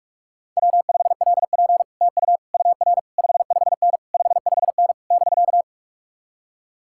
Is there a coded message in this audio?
Yes: Morse code from 0.6 to 5.6 s, heard as a keyed tone, about 57 wpm, 700 Hz. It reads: W5LP TU UR 55N 55N <BK>